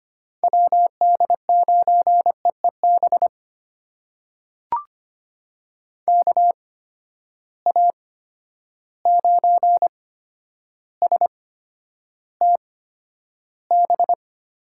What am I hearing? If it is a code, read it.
Morse code, heard as a keyed tone, about 25 wpm, spaced as for 11 wpm, 700 Hz. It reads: WD9EEB KA9STB
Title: USA Call Signs - 6 Characters - 8x Character Spacing - 25wpm